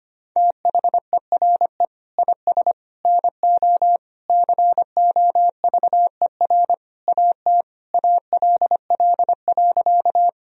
Morse code, heard as a keyed tone, about 25 wpm, 700 Hz. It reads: THERE IS NO COVER AT ALL.
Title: Sentences from Top 1000 Words - Review Part 1 - 25wpm